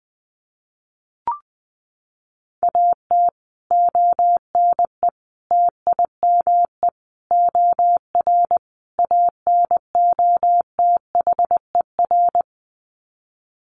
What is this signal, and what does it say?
Morse code, heard as a keyed tone, about 20 wpm, 700 Hz. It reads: AT ONE TIME OR ANOTHER
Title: Sets of 5 Words Encore 20wpm